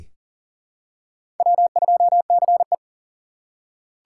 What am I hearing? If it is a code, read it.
Morse code, heard as a keyed tone, about 39 wpm, 700 Hz. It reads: W2CE
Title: USA Call Signs - 4 Characters - 3x Character Spacing - 40wpm